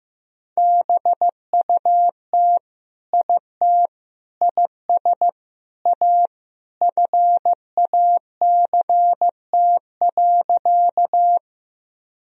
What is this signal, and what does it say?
Morse code, heard as a keyed tone, about 15 wpm, 700 Hz. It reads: BUT IT IS A FACT.
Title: Sentences from Top 1000 Words - Review Part 3 - 15wpm